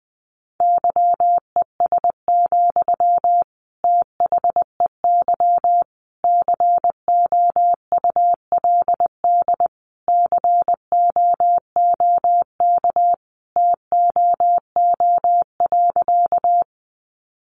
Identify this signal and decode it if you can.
Morse code, heard as a keyed tone, about 20 wpm, 700 Hz. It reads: YES, THEY COULD COOK TOO.
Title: Sentences from Top 700 Words 20wpm